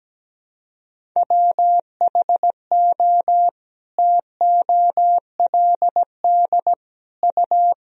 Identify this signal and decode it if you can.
Morse code, heard as a keyed tone, about 17 wpm, 700 Hz. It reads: WHO TOLD U
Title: Sentences from Top 400 Words and Abbreviations 17wpm